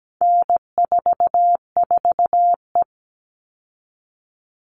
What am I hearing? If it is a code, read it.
Morse code, heard as a keyed tone, about 17 wpm, 700 Hz. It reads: N44E
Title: All Cut Numbers - 4-Digit 17wpm